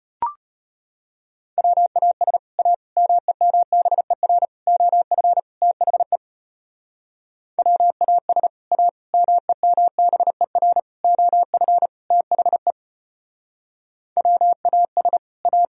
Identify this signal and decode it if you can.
Morse code, heard as a keyed tone, about 32 wpm, 700 Hz. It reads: WAS A MEMBER OF THE WAS A MEMBER OF THE WAS A